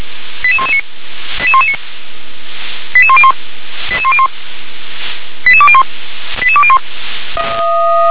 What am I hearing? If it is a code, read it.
Five-tone selective call (ZVEI), heard as quick runs of five tones, tones of about 70 ms. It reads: 88119 88119 88101 88101 88201 88201